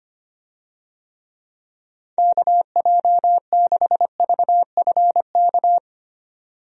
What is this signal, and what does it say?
Morse code, heard as a keyed tone, about 25 wpm, 700 Hz. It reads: KJ6VFK
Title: USA Call Signs - 6 Characters - 25wpm